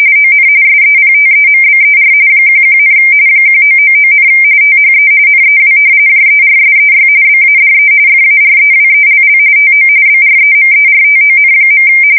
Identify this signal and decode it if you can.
RTTY, about 45.45 baud. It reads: ON SEPTEMBER 11, 1999, THE NEW JERSEY BADE FAREWELL TO BREMERTON.
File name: fsk170 / battleshipnj